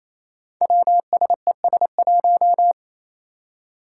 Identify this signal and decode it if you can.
Morse code, heard as a keyed tone, about 28 wpm, 700 Hz. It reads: WSES1
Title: Five Challenging Letter-Number Combinations 28wpm